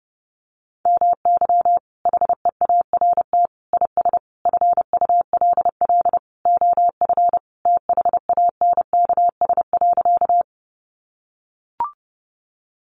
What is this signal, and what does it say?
Morse code, heard as a keyed tone, about 30 wpm, 700 Hz. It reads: MY HEART IS FULL OF THANKS.
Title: Sentences from Top 1200 Words 30wpm